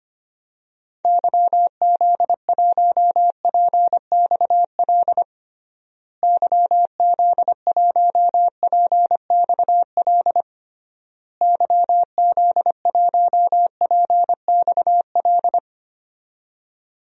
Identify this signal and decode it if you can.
Morse code, heard as a keyed tone, about 25 wpm, 700 Hz. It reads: YZ1PXL YZ1PXL YZ1PXL
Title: DXCC-100 Call Signs - 6 Characters - Repeated 3x - 25wpm